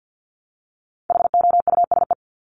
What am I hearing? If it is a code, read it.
Morse code, heard as a keyed tone, about 50 wpm, 700 Hz. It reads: HOUSE
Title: Top 300 Words 50wpm